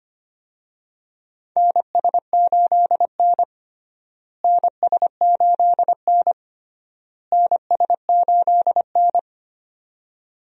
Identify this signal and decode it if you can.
Morse code, heard as a keyed tone, about 25 wpm, 700 Hz. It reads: NS8N NS8N NS8N
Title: USA Call Signs - 4 Characters - Repeated 3x - 25wpm